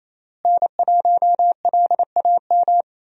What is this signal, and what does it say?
Morse code, heard as a keyed tone, about 28 wpm, 700 Hz. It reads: N1LAM
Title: USA Call Signs - 5 Characters - Repeated 3x - 28wpm